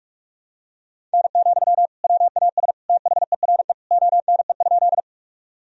Morse code, heard as a keyed tone, about 46 wpm, 700 Hz. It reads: N, WAS THERE ONE?
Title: Sentences from Top 100 Words and Abbreviations 45wpm